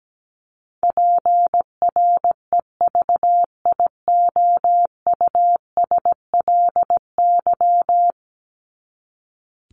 Morse code, heard as a keyed tone, about 17 wpm, 700 Hz. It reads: PREVIOUSLY
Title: LY Suffix - Top 100 Words with Base Words 17wpm